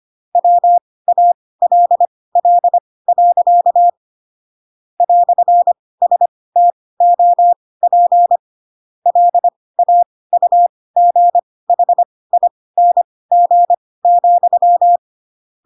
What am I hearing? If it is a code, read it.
Morse code, heard as a keyed tone, about 25 wpm, spaced as for 18 wpm, 700 Hz. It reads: WALL. "STOP LAUGHING,
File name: Pinocchio ch3 12 25 ik1hge